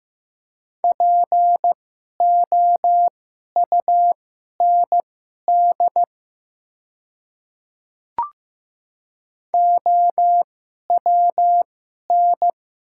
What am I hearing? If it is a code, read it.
Morse code, heard as a keyed tone, about 15 wpm, spaced as for 11 wpm, 700 Hz. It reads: POUND OWN
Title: 2x Character Spacing - Top 500 Words - 15wpm